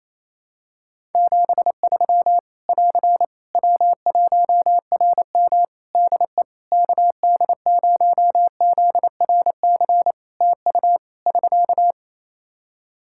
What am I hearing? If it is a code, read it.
Morse code, heard as a keyed tone, about 28 wpm, 700 Hz. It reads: 73 <AR> W1RM DE KD0ZRC TU <SK>
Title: Practice QSO - Exchange 4 - Ending - 28wpm